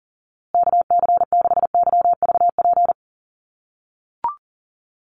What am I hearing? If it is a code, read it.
Morse code, heard as a keyed tone, about 40 wpm, 700 Hz. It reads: KC6YVP
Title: USA Call Signs Repeated 3x 40wpm